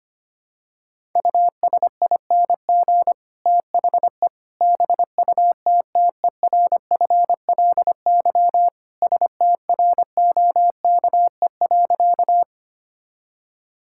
Morse code, heard as a keyed tone, about 25 wpm, 700 Hz. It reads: USING THE BUTTERFLY STROKE.
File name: practice / 25wpm / All-Age Appropriate Jokes - 25wpm